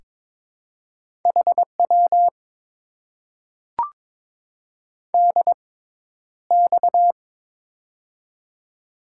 Morse code, heard as a keyed tone, about 22 wpm, spaced as for 11 wpm, 700 Hz. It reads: HW DX
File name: Top 90 QSO Elements - 6x Character Spacing - 22wpm